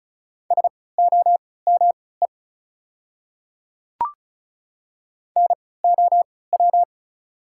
Morse code, heard as a keyed tone, about 36 wpm, spaced as for 20 wpm, 690 Hz. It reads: SOME NOW
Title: 3x Character Spacing - Top 100 Words - 35wpm